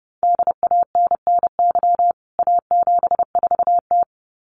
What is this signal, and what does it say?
Morse code, heard as a keyed tone, about 28 wpm, 700 Hz. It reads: DANNY A74T
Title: CWT Exchange 30wpm